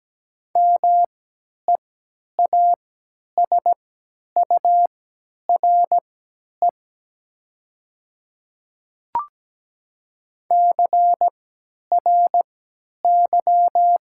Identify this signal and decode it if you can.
Morse code, heard as a keyed tone, about 17 wpm, spaced as for 10 wpm, 700 Hz. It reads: MEASURE CRY
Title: 3x Character Spacing - Top 500 Words - 17wpm